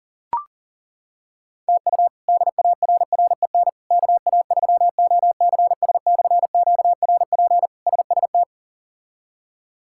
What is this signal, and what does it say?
Morse code, heard as a keyed tone, about 40 wpm, 700 Hz. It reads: TU DARREN KA3OCS/QRP SST
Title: K1USN SST QSO Practice - 40wpm